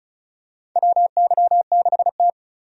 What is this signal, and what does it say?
Morse code, heard as a keyed tone, about 36 wpm, 700 Hz. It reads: WY6T
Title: USA Call Signs - 4 Characters - 35wpm